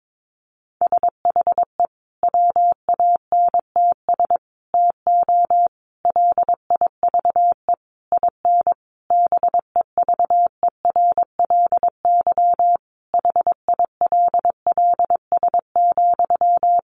Morse code, heard as a keyed tone, about 22 wpm, 700 Hz. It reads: SHE WANTS TO LIVE IN BEVERLY HILLS,
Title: Easy Phrases 22wpm